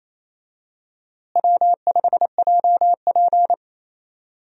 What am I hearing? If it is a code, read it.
Morse code, heard as a keyed tone, about 28 wpm, 700 Hz. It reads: W5JP
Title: USA Call Signs - 4 Characters - Repeated 3x - 28wpm